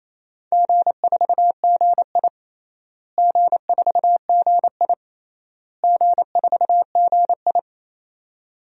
Morse code, heard as a keyed tone, about 28 wpm, 700 Hz. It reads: G4GI G4GI G4GI
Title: DXCC-100 Call Signs - 4 Characters - Repeated 3x - 28wpm